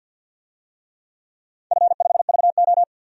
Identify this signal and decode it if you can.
Morse code, heard as a keyed tone, about 52 wpm, 700 Hz. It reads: F5VO
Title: DXCC-100 Call Signs - 4 Characters - 50wpm